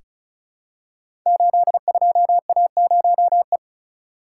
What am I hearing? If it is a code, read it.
Morse code, heard as a keyed tone, about 36 wpm, 700 Hz. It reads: 82A0E
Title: Code Groups - 5 Letter:Numbers - 35wpm